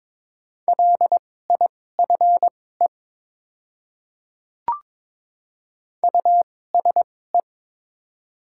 Morse code, heard as a keyed tone, about 22 wpm, spaced as for 16 wpm, 700 Hz. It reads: LIFE USE
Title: 2x Character Spacing - Top 300 Words - 22wpm